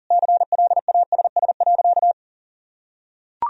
Morse code, heard as a keyed tone, about 40 wpm, 700 Hz. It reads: CLASS.
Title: Sentences from Top 1000 Words - Review Part 2 - 40wpm